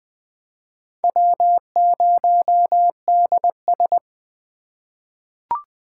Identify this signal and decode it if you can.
Morse code, heard as a keyed tone, about 20 wpm, 700 Hz. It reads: W0DS
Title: USA Call Signs - 4 Characters - 2x Character Spacing - 20wpm